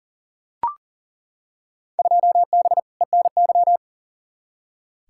Morse code, heard as a keyed tone, about 40 wpm, 700 Hz. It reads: 2B ENY
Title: Field Day Exchanges Encore - 40wpm